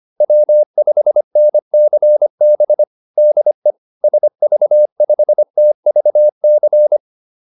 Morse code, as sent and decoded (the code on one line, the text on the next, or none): .-- ..... -. -.-. -... / -.. / ... ...- ..... - ...- -.-.
W5NCB DE SV5TVC